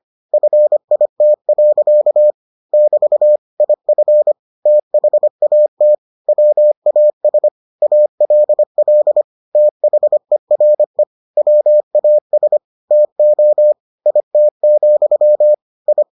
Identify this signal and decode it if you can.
Morse code, heard as a keyed tone, about 25 wpm, 600 Hz. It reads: FIT. <BT> IF THAT WAS ALL THERE WAS TO IT, I